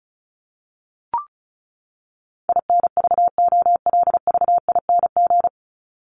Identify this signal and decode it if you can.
Morse code, heard as a keyed tone, about 34 wpm, 700 Hz. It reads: INVOLVING
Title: ING Suffix - Top 100 Words with Base Words 35wpm